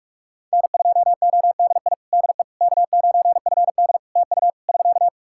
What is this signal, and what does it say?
Morse code, heard as a keyed tone, about 42 wpm, 700 Hz. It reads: N2ODI DE K9FD TU <SK>
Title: Practice QSO - Exchange 4 - Ending - 45wpm